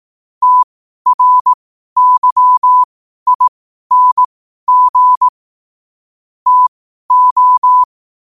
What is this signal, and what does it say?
Morse code, heard as a keyed tone, about 18 wpm, spaced as for 12 wpm, 1000 Hz. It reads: TRYING TO
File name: qotdfiles / 12wpm